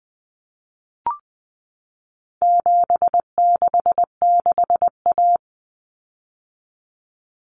Morse code, heard as a keyed tone, about 20 wpm, 700 Hz. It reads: 766A
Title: Common Cut Numbers - 4-Digit 20wpm